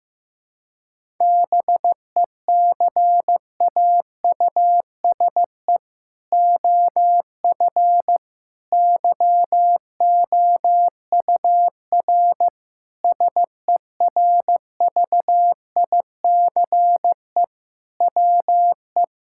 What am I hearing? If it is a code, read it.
Morse code, heard as a keyed tone, about 15 wpm, 700 Hz. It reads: BECAUSE OF YOUR SERVICE WE
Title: Sentences from Top 1000 Words - Review Part 5 - 15wpm